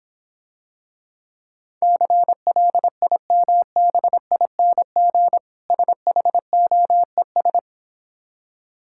Morse code, heard as a keyed tone, about 26 wpm, 690 Hz. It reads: CLIMBING SHOES